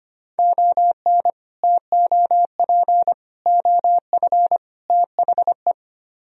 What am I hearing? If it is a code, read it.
Morse code, heard as a keyed tone, about 32 wpm, 700 Hz. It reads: ON TOP OF THE